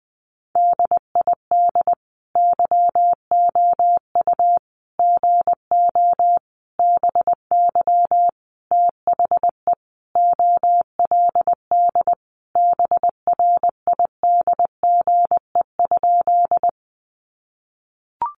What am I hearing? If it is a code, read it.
Morse code, heard as a keyed tone, about 20 wpm, 700 Hz. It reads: DID YOU GO BY THE OLD BRIDGE?
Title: Sentences from Top 1200 Words 20wpm